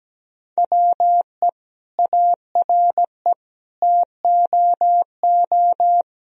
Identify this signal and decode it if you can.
Morse code, heard as a keyed tone, about 17 wpm, 700 Hz. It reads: WE ARE TOO
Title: Sentences from Top 1000 Words - Review Part 4 - 17wpm